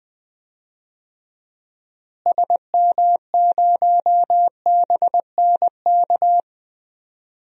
Morse code, heard as a keyed tone, about 20 wpm, 700 Hz. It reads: SM0BNK
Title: DXCC-100 Call Signs - 6 Characters - Repeated 3x - 20wpm